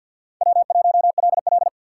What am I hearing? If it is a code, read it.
Morse code, heard as a keyed tone, about 53 wpm, 700 Hz. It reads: W1LL
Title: USA Call Signs - 6x Character Spacing - 50wpm